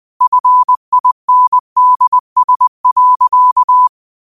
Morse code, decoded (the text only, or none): FINDS.